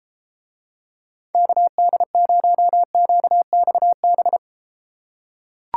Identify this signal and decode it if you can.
Morse code, heard as a keyed tone, about 31 wpm, 700 Hz. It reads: KD0QXB